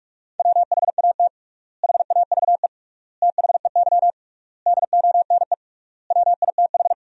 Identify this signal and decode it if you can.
Morse code, heard as a keyed tone, about 41 wpm, 700 Hz. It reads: WHAT HAVE THEY DONE WITH
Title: 3x Word-Spacing - Sentences from Top 500 Words - 45wpm